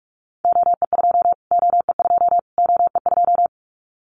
Morse code, heard as a keyed tone, about 44 wpm, 700 Hz. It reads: OE2 OE2 OE2
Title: Callsign Prefixes for Top 25 DXCC - Repeated 3x - 45wpm